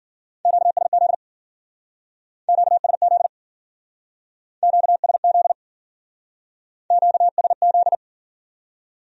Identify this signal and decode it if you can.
Morse code, heard as a keyed tone, about 50 wpm, 700 Hz. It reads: QSZ QSZ QSZ QSZ